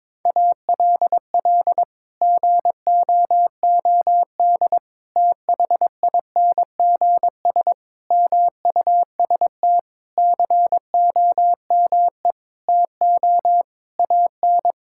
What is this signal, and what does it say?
Morse code, heard as a keyed tone, about 21 wpm, 700 Hz. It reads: ALL GOOD THINGS MUST COME TO AN
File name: practice / 22wpm / Sets of 8 Words - 22wpm